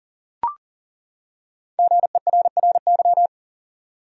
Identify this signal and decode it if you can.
Morse code, heard as a keyed tone, about 40 wpm, 700 Hz. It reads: GERRY
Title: USA Common Names - Max 5 Letters - 40wpm